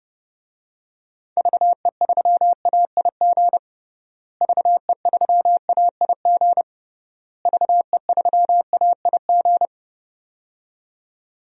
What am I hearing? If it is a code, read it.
Morse code, heard as a keyed tone, about 30 wpm, 700 Hz. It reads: VE3AIG VE3AIG VE3AIG
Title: DXCC-100 Call Signs - 6 Characters - Repeated 3x - 30wpm